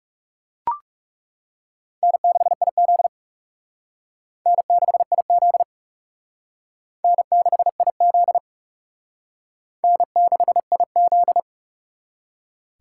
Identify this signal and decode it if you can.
Morse code, heard as a keyed tone, about 37 wpm, 700 Hz. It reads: N6IZ N6IZ N6IZ N6IZ
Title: Speed Racing - USA 1x2 Call Signs 30wpm